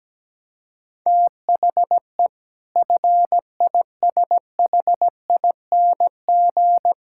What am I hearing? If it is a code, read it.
Morse code, heard as a keyed tone, about 17 wpm, 700 Hz. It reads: THE FISHING
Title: Sentences from Top 1600 Words 17wpm